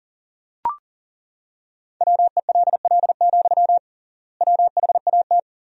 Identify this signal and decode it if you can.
Morse code, heard as a keyed tone, about 40 wpm, 700 Hz. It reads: WELL, WHAT
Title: 3x Word-Spacing - Sentences from Top 300 Words - 40wpm